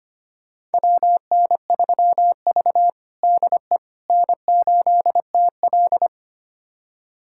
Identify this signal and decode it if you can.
Morse code, heard as a keyed tone, about 25 wpm, 700 Hz. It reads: WN3V DE N8TL
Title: USA Callsign DE Callsign - 4 Characters - 25wpm